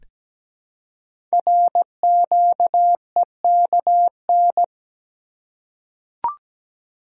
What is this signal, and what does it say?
Morse code, heard as a keyed tone, about 17 wpm, 700 Hz. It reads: RQEKN
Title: Code Groups - 5 Letters - 17wpm